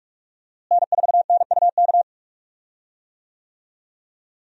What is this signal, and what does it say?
Morse code, heard as a keyed tone, about 47 wpm, 700 Hz. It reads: N4NUK
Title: USA Call Signs - 5 Characters - Repeated 2x - 45wpm